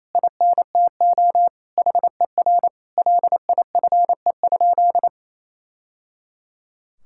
Morse code, heard as a keyed tone, about 27 wpm, 700 Hz. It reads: INTO HER LIFE?
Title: Sentences from Top 1000 Words - Review Part 6 - 28wpm